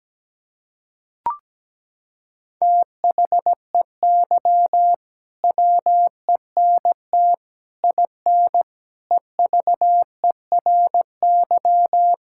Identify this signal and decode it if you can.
Morse code, heard as a keyed tone, about 17 wpm, 700 Hz. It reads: THEY WENT IN EVERY
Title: Sentences from Top 1000 Words - Review Part 5 - 17wpm